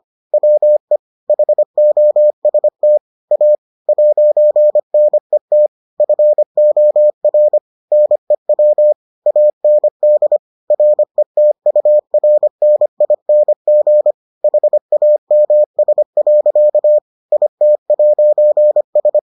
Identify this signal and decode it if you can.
Morse code, heard as a keyed tone, about 25 wpm, 600 Hz. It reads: WE HOST A 'NET FOR NEW AND RETURNING HAMS. IT'S